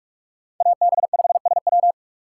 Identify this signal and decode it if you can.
Morse code, heard as a keyed tone, about 46 wpm, 700 Hz. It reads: AB5SW